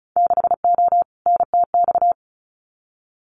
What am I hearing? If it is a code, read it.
Morse code, heard as a keyed tone, about 34 wpm, 700 Hz. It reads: 6O NTX
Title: Winter Field Day Exchanges Encore 35wpm